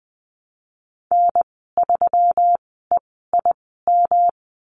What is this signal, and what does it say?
Morse code, heard as a keyed tone, about 20 wpm, spaced as for 15 wpm, 700 Hz. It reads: N3EIM
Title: 128 Lesson 21g - Number 6 - Call Signs 20wpm